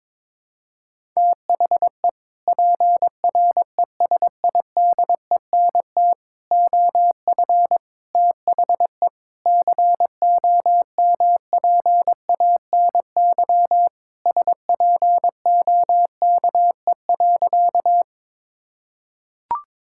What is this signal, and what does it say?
Morse code, heard as a keyed tone, about 22 wpm, 700 Hz. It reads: THE PRESIDENT OF THE COMPANY SPOKE.
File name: practice / 22wpm / Sentences with 3-Word Phrases - 22wpm